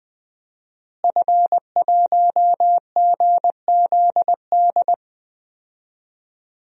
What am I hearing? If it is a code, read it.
Morse code, heard as a keyed tone, about 20 wpm, 700 Hz. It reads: F1GZD
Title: DXCC-100 Call Signs - 5 Characters - Repeated 2x - 20wpm